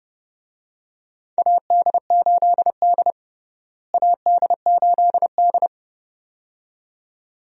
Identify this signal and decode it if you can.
Morse code, heard as a keyed tone, about 30 wpm, 700 Hz. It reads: AD8D AD8D
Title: USA Call Signs - 4 Characters - Repeated 2x - 30wpm